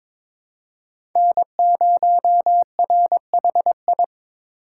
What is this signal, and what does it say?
Morse code, heard as a keyed tone, about 22 wpm, 700 Hz. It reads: N0RHI
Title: USA Call Signs - 3x Character Spacing - 22wpm